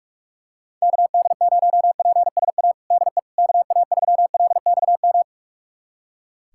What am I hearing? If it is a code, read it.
Morse code, heard as a keyed tone, about 41 wpm, 700 Hz. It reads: KD0PSA DE KA3LXM
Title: USA Callsign DE Callsign - 6 Characters - 45wpm